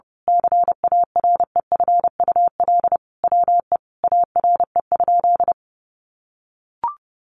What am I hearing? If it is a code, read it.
Morse code, heard as a keyed tone, about 30 wpm, 700 Hz. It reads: CAREFUL WE ARE?
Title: A Sound Of Thunder 30wpm